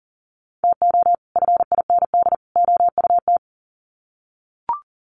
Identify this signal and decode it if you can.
Morse code, heard as a keyed tone, about 40 wpm, 700 Hz. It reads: TO FIND OUT
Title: Sets of 3 Words 40wpm